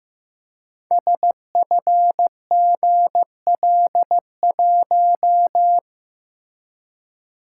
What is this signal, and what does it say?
Morse code, heard as a keyed tone, about 15 wpm, 700 Hz. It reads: SFGL1
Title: Code Groups - 5 Letter:Numbers - 15wpm